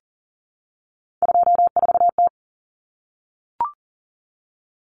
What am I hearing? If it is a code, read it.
Morse code, heard as a keyed tone, about 40 wpm, 700 Hz. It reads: 24T
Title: Common Cut Numbers - 3-Digit 40wpm